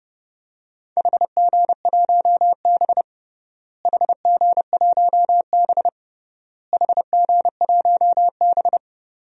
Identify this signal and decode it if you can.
Morse code, heard as a keyed tone, about 30 wpm, 700 Hz. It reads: HG1B HG1B HG1B
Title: DXCC-100 Call Signs - 4 Characters - Repeated 3x - 30wpm